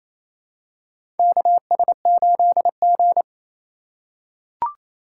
Morse code, heard as a keyed tone, about 28 wpm, 700 Hz. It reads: KS8G
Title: USA Call Signs - 4x Character Spacing - 28wpm